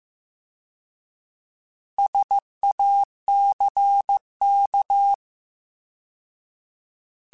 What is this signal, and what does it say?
Morse code, heard as a keyed tone, about 15 wpm, 800 Hz. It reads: SACK
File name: Alphabetical / Alphabetical mp3s / Ss @13